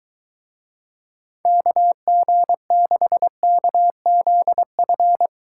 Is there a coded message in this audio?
Yes: Morse code from 1.4 to 5.3 s, heard as a keyed tone, about 23 wpm, 700 Hz. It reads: KG6KZF